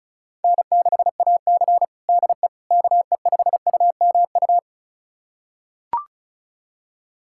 Morse code, heard as a keyed tone, about 36 wpm, 700 Hz. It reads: N6AC DE KE5UMU
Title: USA Callsign DE Callsign 35wpm